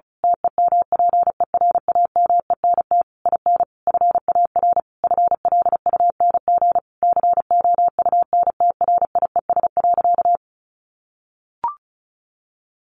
Morse code, heard as a keyed tone, about 34 wpm, 700 Hz. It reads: TEMPERAMENT IN FAR FLUNG COUNTRIES.
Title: A Sound Of Thunder 35wpm